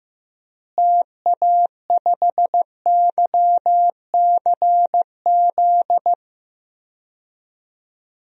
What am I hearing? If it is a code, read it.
Morse code, heard as a keyed tone, about 15 wpm, 700 Hz. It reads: TA5YCZ
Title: DXCC-100 Call Signs - 6 Characters - Repeated 3x - 15wpm